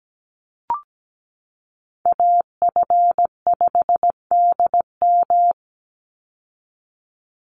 Morse code, heard as a keyed tone, about 17 wpm, 700 Hz. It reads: AF5DM
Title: Call Signs - 17wpm